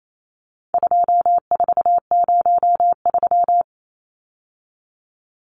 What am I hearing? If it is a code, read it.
Morse code, heard as a keyed tone, about 28 wpm, 700 Hz. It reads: 2403
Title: Numbers - 4-Digit 28wpm